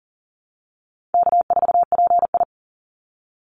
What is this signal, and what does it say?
Morse code, heard as a keyed tone, about 40 wpm, 700 Hz. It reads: K4PI
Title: Easy Call Signs Repeated 2x - 40wpm